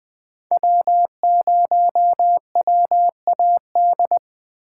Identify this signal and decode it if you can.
Morse code, heard as a keyed tone, about 24 wpm, 700 Hz. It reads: W0WAD
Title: Speed Racing - USA Call Signs 20wpm